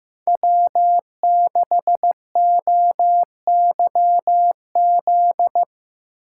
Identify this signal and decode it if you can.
Morse code, heard as a keyed tone, about 16 wpm, 700 Hz. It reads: W6OYZ
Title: Speed Racing - USA 1x3 Call Signs 15wpm